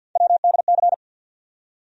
Morse code, heard as a keyed tone, about 53 wpm, 700 Hz. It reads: WDC
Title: USA Call Sign Postfixes - 3 Letters 50wpm